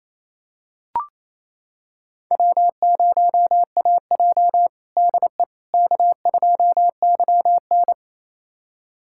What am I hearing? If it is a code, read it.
Morse code, heard as a keyed tone, about 28 wpm, 700 Hz. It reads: W0AJ DE K2YN
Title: USA Callsign DE Callsign - 4 Characters - 28wpm